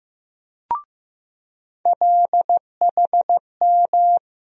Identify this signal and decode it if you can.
Morse code, heard as a keyed tone, about 15 wpm, 700 Hz. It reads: LHM
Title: USA Call Sign Postfixes - 3 Letters 15wpm